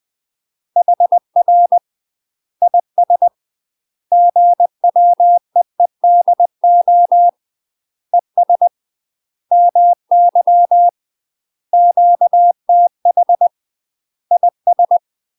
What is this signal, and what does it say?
Morse code, heard as a keyed tone, about 20 wpm, 700 Hz. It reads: HR IS GWEEDO ES MY QTH IS
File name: UK MASTER QSOs TRAINING / 20WPM / UK MASTER QSOs 0068